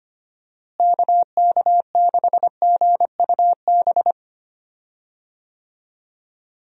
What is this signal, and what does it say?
Morse code, heard as a keyed tone, about 25 wpm, 700 Hz. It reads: KK6GUB
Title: USA Call Signs - 6 Characters - Repeated 3x - 25wpm